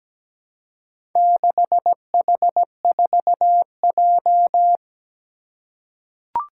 Morse code, heard as a keyed tone, about 17 wpm, 700 Hz. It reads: 6H4J
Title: Four Challenging Letter-Number Combinations 17wpm